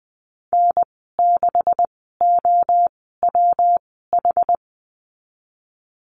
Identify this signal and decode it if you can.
Morse code, heard as a keyed tone, about 20 wpm, spaced as for 15 wpm, 700 Hz. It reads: N6OWH
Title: 167 Lesson 27g - Number 7 - Call Signs 20wpm